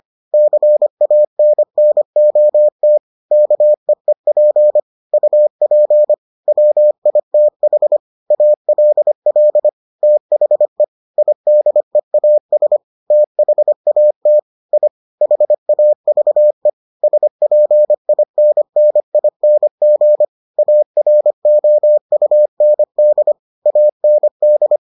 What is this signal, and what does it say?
Morse code, heard as a keyed tone, about 25 wpm, 600 Hz. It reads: CANNOT KEEP UP WITH ALL THE IDEAS THAT I HAVE SPINNING AROUND AND